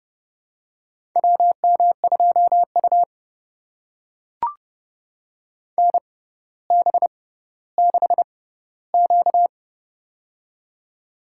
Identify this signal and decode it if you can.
Morse code, heard as a keyed tone, about 31 wpm, spaced as for 16 wpm, 700 Hz. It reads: WM2U NB6Q
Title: USA Call Signs - 4 Characters - 6x Character Spacing - 30wpm